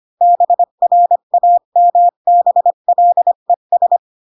Morse code, heard as a keyed tone, about 25 wpm, 700 Hz. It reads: BRAMBLES
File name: Pinocchio / Pinocchio ch4 20 25 ik1hge